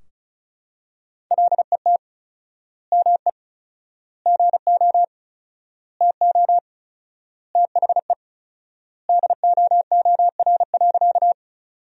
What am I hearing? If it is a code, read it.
Morse code, heard as a keyed tone, about 36 wpm, 700 Hz. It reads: LET ME GO TO THE DOOR.
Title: 4x Word-Spacing - Sentences from Top 300 Words - 35wpm